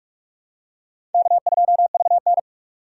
Morse code, heard as a keyed tone, about 46 wpm, 700 Hz. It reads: K2VN
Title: USA Call Signs - 4 Characters - 45wpm